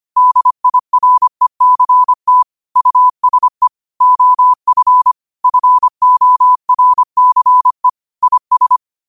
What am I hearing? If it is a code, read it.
Morse code, heard as a keyed tone, about 25 wpm, 1000 Hz. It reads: DIRECT USE OF FORCE IS
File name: qotdfiles / 25wpm